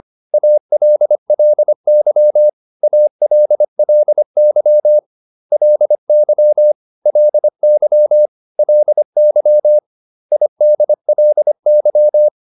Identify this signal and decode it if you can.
Morse code, heard as a keyed tone, about 25 wpm, 600 Hz. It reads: ALLY ALLY LY LY LY IDLY